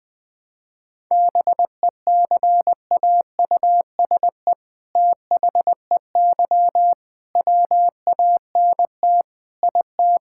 Morse code, heard as a keyed tone, about 20 wpm, 700 Hz. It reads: BECAUSE THEY WANT IT
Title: Sentences from Top 1000 Words - Review Part 5 - 20wpm